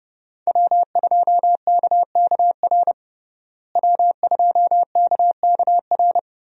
Morse code, heard as a keyed tone, about 30 wpm, 700 Hz. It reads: W2KKR W2KKR
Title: USA Call Signs - 5 Characters - Repeated 2x - 30wpm